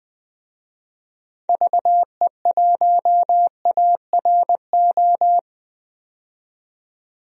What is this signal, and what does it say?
Morse code, heard as a keyed tone, about 20 wpm, 700 Hz. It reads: VE1ARO
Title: DXCC-100 Call Signs - 6 Characters - 20wpm